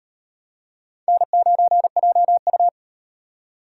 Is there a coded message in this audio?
Yes: Morse code from 1.1 to 2.7 s, heard as a keyed tone, about 36 wpm, 700 Hz. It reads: N9JU